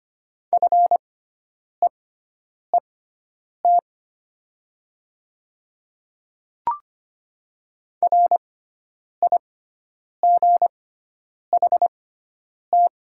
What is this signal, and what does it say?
Morse code, heard as a keyed tone, about 25 wpm, spaced as for 9 wpm, 700 Hz. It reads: FEET RIGHT
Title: 6x Character Spacing - Top 300 Words - 25wpm